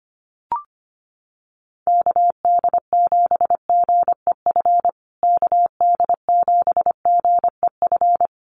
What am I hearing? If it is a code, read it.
Morse code, heard as a keyed tone, about 25 wpm, 700 Hz. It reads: KD7GEF KD7GEF
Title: USA Call Signs Repeated 3x 25wpm